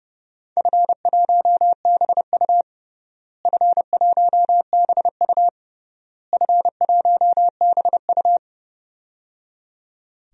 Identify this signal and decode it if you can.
Morse code, heard as a keyed tone, about 31 wpm, 700 Hz. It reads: F1BU F1BU F1BU
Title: DXCC-100 Call Signs - 4 Characters - Repeated 3x - 30wpm